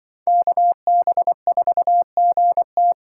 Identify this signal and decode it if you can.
Morse code, heard as a keyed tone, about 23 wpm, 700 Hz. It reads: KB4GT
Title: Speed Racing - USA 2x2 Call Signs 20wpm